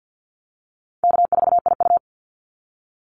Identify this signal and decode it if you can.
Morse code, heard as a keyed tone, about 50 wpm, 700 Hz. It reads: K4IU
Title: Call Signs - 50wpm